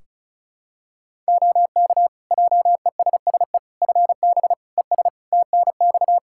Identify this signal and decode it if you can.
Morse code, heard as a keyed tone, about 34 wpm, 700 Hz. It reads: OK JESSE FB ES TNX
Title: Practice QSO - Exchange 2 - Info - 35wpm